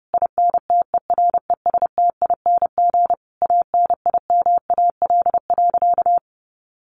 Morse code, heard as a keyed tone, about 30 wpm, 700 Hz. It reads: INTERESTING ANIMAL.
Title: Easy Phrases 30wpm